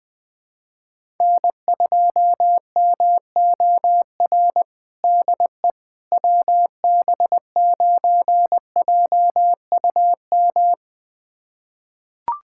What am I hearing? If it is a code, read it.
Morse code, heard as a keyed tone, about 21 wpm, 700 Hz. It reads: N2MOR DE WB9JUM
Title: USA Callsign DE Callsign 20wpm